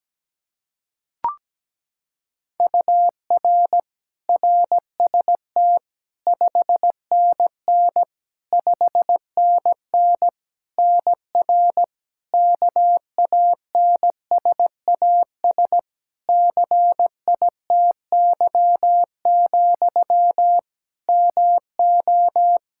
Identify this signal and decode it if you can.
Morse code, heard as a keyed tone, about 17 wpm, 700 Hz. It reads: UR RST 5NN 5NN NR KANSAS CITY, MO
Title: Practice QSO - Exchange 1 - Beginning - 17wpm